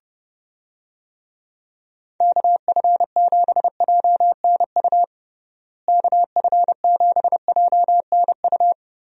Morse code, heard as a keyed tone, about 30 wpm, 700 Hz. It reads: KF7JNU KF7JNU
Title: USA Call Signs - 6 Characters - Repeated 2x - 30wpm